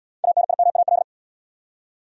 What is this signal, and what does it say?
Morse code, heard as a keyed tone, about 73 wpm, 700 Hz. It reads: ANIMAL